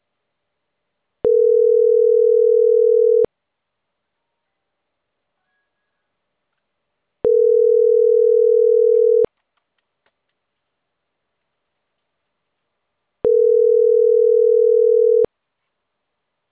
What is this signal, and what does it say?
Ringback tone (North America), 440 + 480 Hz.
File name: CENTRAL CAEA / out agcaea06